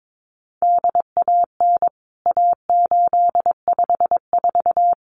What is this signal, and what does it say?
Morse code, heard as a keyed tone, about 22 wpm, 700 Hz. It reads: DAN A854
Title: CWT Exchange 22wpm